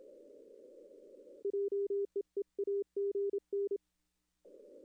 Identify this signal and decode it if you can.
Morse code, heard as a keyed tone, about 27 wpm, 410 Hz. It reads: JEE AGN
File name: contests / arrl-160-k5tr-op-w5tn -2300